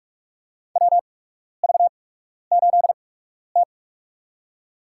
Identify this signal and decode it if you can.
Morse code, heard as a keyed tone, about 43 wpm, spaced as for 12 wpm, 700 Hz. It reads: WV8T
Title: USA Call Signs - 4 Characters - 8x Character Spacing - 45wpm